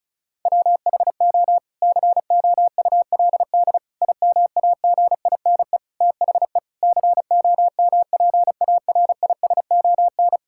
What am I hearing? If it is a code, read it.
Morse code, heard as a keyed tone, about 34 wpm, 700 Hz. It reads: WHO COULD IMAGINE THE COMPARISON
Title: Sentences from Top 1600 Words 35wpm